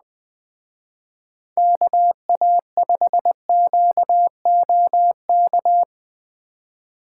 Morse code, heard as a keyed tone, about 20 wpm, 700 Hz. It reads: KA5QOK